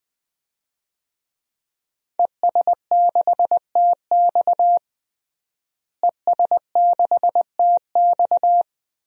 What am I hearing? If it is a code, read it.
Morse code, heard as a keyed tone, about 20 wpm, 700 Hz. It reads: ES6TX ES6TX
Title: DXCC-100 Call Signs - 5 Characters - Repeated 2x - 20wpm